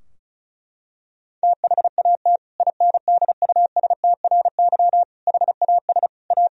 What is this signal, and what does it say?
Morse code, heard as a keyed tone, about 36 wpm, 700 Hz. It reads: THAT INDUSTRY HAS A
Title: Sentences from Top 1000 Words - Review Part 4 - 35wpm